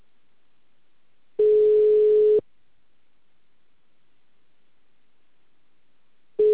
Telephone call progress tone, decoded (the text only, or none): ringback tone